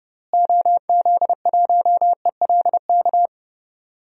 Morse code, heard as a keyed tone, about 31 wpm, 700 Hz. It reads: OZ1ELK
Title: DXCC-100 Call Signs - 6 Characters - Repeated 2x - 30wpm